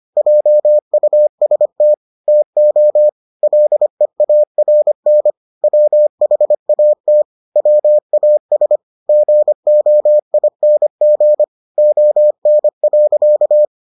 Morse code, heard as a keyed tone, about 25 wpm, 600 Hz. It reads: JUST TO LEARN WHAT WAS GOING ON.